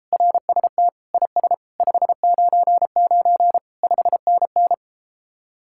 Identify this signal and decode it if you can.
Morse code, heard as a keyed tone, about 33 wpm, 700 Hz. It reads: RST IS 599 5NN